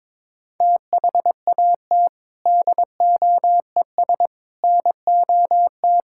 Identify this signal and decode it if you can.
Morse code, heard as a keyed tone, about 22 wpm, 700 Hz. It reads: THAT DOES NOT
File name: practice / 22wpm / Sentences with 4-Word Phrases - 22wpm